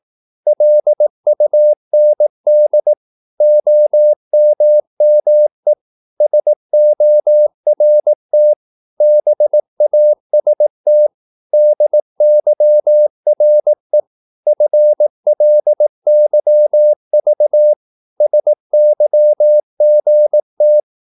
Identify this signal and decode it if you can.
Morse code, heard as a keyed tone, about 18 wpm, 600 Hz. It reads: LUND OMME SORT BAST DYRE FLYV SYGT
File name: Korte ord DK 18wpm